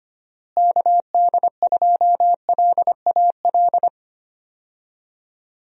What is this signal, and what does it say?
Morse code, heard as a keyed tone, about 25 wpm, 700 Hz. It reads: KD2LAL
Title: USA Call Signs - 6 Characters - Repeated 2x - 25wpm